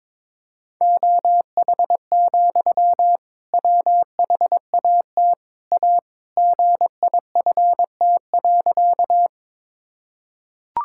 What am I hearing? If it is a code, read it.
Morse code, heard as a keyed tone, about 21 wpm, 700 Hz. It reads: OH, WHAT A GIFT.
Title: Sentences from Top 1000 Words - Review Part 6 - 22wpm